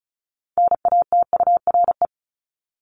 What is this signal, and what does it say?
Morse code, heard as a keyed tone, about 34 wpm, 700 Hz. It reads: NATURE
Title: Top 501-750 Words 35wpm